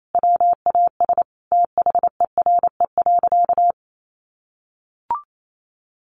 Morse code, heard as a keyed tone, about 28 wpm, 700 Hz. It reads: WAS THERE.
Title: Sentences from Top 1400 Words 28wpm